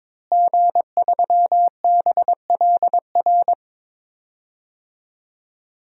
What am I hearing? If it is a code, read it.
Morse code, heard as a keyed tone, about 21 wpm, 700 Hz. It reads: G3BLR
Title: DXCC-100 Call Signs - 5 Characters - Repeated 3x - 22wpm